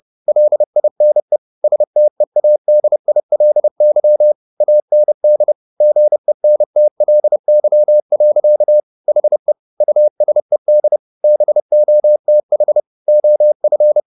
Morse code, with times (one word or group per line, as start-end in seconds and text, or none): 0.3-1.4 LINE
1.6-4.3 STEADILY
4.6-5.5 AND
5.8-8.8 GENTLY.
9.1-9.5 HE
9.8-11.0 USED
11.2-12.8 BOTH
13.1-14.0 OF